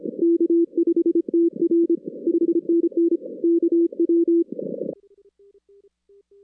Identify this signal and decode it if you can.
Morse code, heard as a keyed tone, about 26 wpm, 340 Hz. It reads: K5TR 5NN KW
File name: 2016 arrl dx cw / 2016-ARRL-DX-CW-K5TR- - 0-41